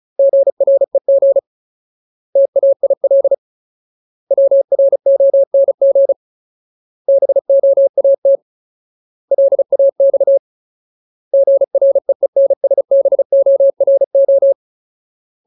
Morse code, heard as a keyed tone, about 34 wpm, 550 Hz. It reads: GREG TAIL WRONG BOAT LAX GREENSBORO
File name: Ultimate Word List / 35WPM Ultimate